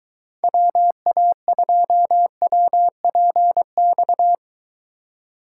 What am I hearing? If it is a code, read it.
Morse code, heard as a keyed tone, about 24 wpm, 700 Hz. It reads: WA2WPX